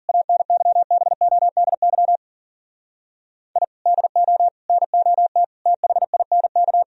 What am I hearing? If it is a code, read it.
Morse code, heard as a keyed tone, about 52 wpm, 700 Hz. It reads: ANYBODY I DO NOT THINK